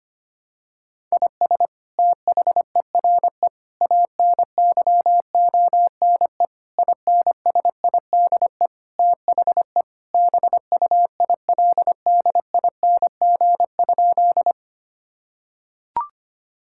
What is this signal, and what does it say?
Morse code, heard as a keyed tone, about 25 wpm, 700 Hz. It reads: IS THERE ANYONE INSIDE THE BUILDING?
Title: Sentences from Top 1000 Words - Review Part 6 - 25wpm